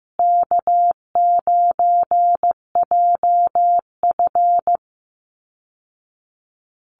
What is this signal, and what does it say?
Morse code, heard as a keyed tone, about 15 wpm, 700 Hz. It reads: K9JF
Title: Easy Call Signs Repeated 3x - 15wpm